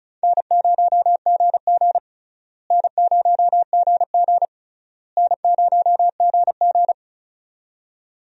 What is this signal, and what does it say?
Morse code, heard as a keyed tone, about 36 wpm, 700 Hz. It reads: N0GG N0GG N0GG
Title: USA Call Signs - 4 Characters - Repeated 3x - 35wpm